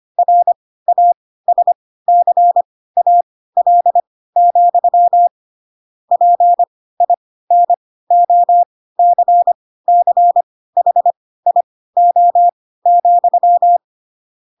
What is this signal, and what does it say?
Morse code, heard as a keyed tone, about 25 wpm, spaced as for 16 wpm, 700 Hz. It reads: RASCAL, PINOCCHIO,
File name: Pinocchio / Pinocchio ch4 10 25 ik1hge